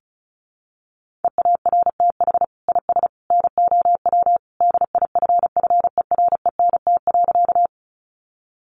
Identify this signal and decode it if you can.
Morse code, heard as a keyed tone, about 34 wpm, 700 Hz. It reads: EARTH IS NOW DIFFERENT.
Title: Sentences from Top 600 Words 35wpm